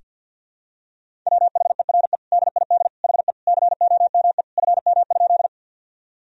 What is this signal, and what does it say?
Morse code, heard as a keyed tone, about 47 wpm, 700 Hz. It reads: WHERE DID HE COME FM?
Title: Sentences from Top 200 Words and Abbreviations 50wpm